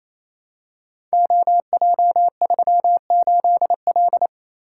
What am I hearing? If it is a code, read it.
Morse code, heard as a keyed tone, about 27 wpm, 700 Hz. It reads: OJ38L
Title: Code Groups - 5 Letter:Numbers - 28wpm